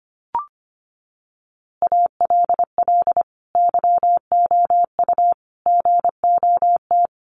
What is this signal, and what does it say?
Morse code, heard as a keyed tone, about 25 wpm, 700 Hz. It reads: ALL YOU GOT
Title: A Sound Of Thunder 25wpm